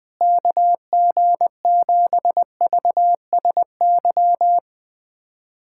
Morse code, heard as a keyed tone, about 20 wpm, 700 Hz. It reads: KG7VSY
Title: USA Callsign DE Callsign 20wpm